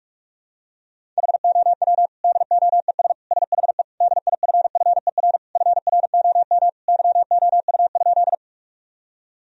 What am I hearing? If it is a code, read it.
Morse code, heard as a keyed tone, about 41 wpm, 700 Hz. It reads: HOW DOES SHE DIFFER FROM YOU?
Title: Sentences from Top 1000 Words - Review Part 1 - 45wpm